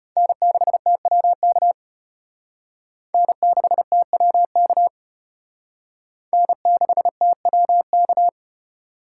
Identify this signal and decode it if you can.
Morse code, heard as a keyed tone, about 31 wpm, 700 Hz. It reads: N6TWK N6TWK N6TWK
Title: Speed Racing - USA 1x3 Call Signs 25wpm